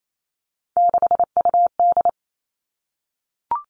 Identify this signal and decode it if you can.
Morse code, heard as a keyed tone, about 28 wpm, 700 Hz. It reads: 6UD
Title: Three Challenging Letter-Number Combinations 28wpm